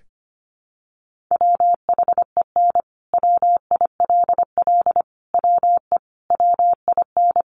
Morse code, heard as a keyed tone, about 25 wpm, 700 Hz. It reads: WHEN WILL WE WIN
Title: Sentences from Top 1500 Words 25wpm